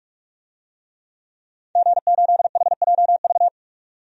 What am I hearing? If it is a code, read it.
Morse code, heard as a keyed tone, about 42 wpm, 700 Hz. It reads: G8HJV
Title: DXCC-100 Call Signs - 5 Characters - 45wpm